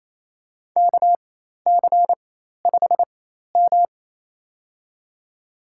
Morse code, heard as a keyed tone, about 28 wpm, spaced as for 13 wpm, 700 Hz. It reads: KC5M
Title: USA Call Signs - 4x Character Spacing - 28wpm